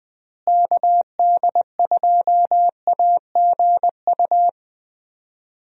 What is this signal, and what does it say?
Morse code, heard as a keyed tone, about 21 wpm, 700 Hz. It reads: KD2AGU
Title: USA Call Signs - 6 Characters - Repeated 2x - 20wpm